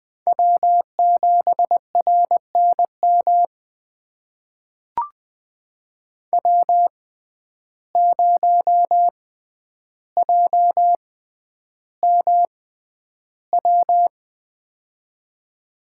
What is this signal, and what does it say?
Morse code, heard as a keyed tone, about 21 wpm, spaced as for 10 wpm, 700 Hz. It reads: W7RNM W0JMW
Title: USA Call Signs - 5 Characters - 6x Character Spacing - 20wpm